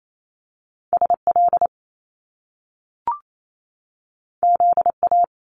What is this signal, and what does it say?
Morse code, heard as a keyed tone, about 28 wpm, 700 Hz. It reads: SL ZA
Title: Top Two Letter Combinations in Words 28wpm